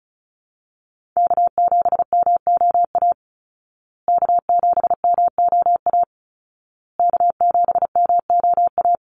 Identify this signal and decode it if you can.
Morse code, heard as a keyed tone, about 34 wpm, 700 Hz. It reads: K7MOA K7MOA K7MOA
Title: Call Signs Repeated 2x - 35wpm